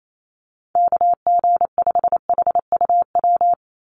Morse code, heard as a keyed tone, about 28 wpm, 700 Hz. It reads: KG5HUW
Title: USA Call Signs 28wpm